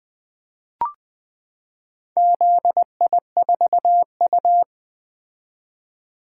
Morse code, heard as a keyed tone, about 21 wpm, 700 Hz. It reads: ZI4U
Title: Four Letter-Number Combinations - 20wpm